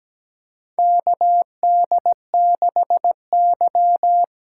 Morse code, heard as a keyed tone, about 17 wpm, 700 Hz. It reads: KD6Y
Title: USA Call Signs - 6 Characters - 8x Character Spacing - 17wpm